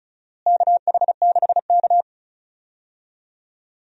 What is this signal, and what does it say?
Morse code, heard as a keyed tone, about 34 wpm, 700 Hz. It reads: KH6K
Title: DXCC-100 Call Signs - 4 Characters - Repeated 3x - 35wpm